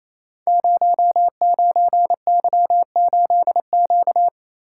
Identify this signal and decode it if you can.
Morse code, heard as a keyed tone, about 27 wpm, 700 Hz. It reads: 09Y8Q
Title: Code Groups - 5 Letter:Numbers - 28wpm